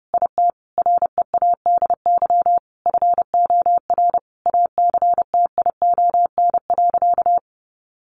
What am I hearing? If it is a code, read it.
Morse code, heard as a keyed tone, about 30 wpm, 700 Hz. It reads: IT READY FOR ACTION.
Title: Sentences from Top 1300 Words 30wpm